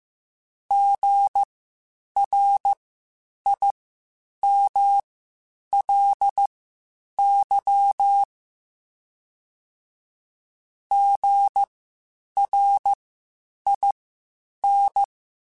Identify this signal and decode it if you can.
Morse code, heard as a keyed tone, about 15 wpm, spaced as for 8 wpm, 800 Hz. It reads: GRIMLY GRIN
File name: Spaced Alphas / Spaced Alphas mp3s / Gs sp @13